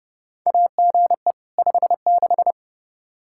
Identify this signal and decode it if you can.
Morse code, heard as a keyed tone, about 30 wpm, 700 Hz. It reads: AGE 56
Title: K1USN SST QSO Practice 30wpm